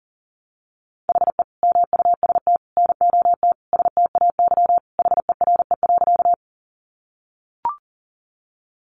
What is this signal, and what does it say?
Morse code, heard as a keyed tone, about 40 wpm, 700 Hz. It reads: HE MUST NOT STAY HERE.
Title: Sentences from Top 600 Words 40wpm